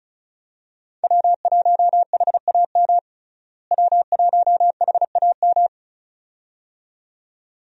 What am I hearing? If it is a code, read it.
Morse code, heard as a keyed tone, about 35 wpm, 700 Hz. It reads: W1HAM W1HAM
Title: USA Call Signs - 5 Characters - Repeated 2x - 35wpm